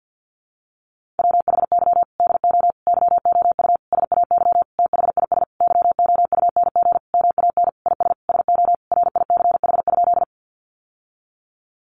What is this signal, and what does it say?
Morse code, heard as a keyed tone, about 50 wpm, 700 Hz. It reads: WHY DO YOU SAY THIS YOUNG MAN IS SO RICH?
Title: Sentences from Top 1100 Words 50wpm